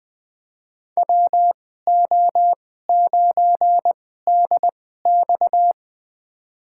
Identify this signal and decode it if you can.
Morse code, heard as a keyed tone, about 21 wpm, spaced as for 15 wpm, 700 Hz. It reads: WO9DX
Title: USA Call Signs - 5 Characters - 2x Character Spacing - 20wpm